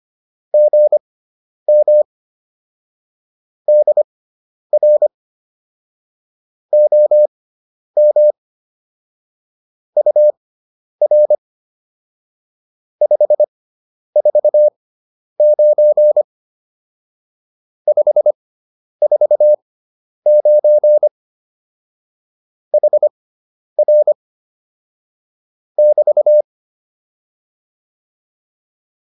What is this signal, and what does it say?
Morse code, heard as a keyed tone, about 25 wpm, spaced as for 10 wpm, 600 Hz. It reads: GM DR OM UR 549 549 HR <BT>